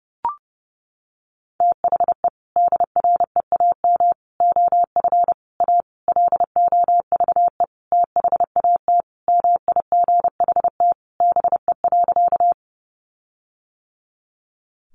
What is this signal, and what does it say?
Morse code, heard as a keyed tone, about 28 wpm, 700 Hz. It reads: THE DREAM OF A LOVE THAT MIGHT BE.
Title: Sentences from Top 1500 Words 30wpm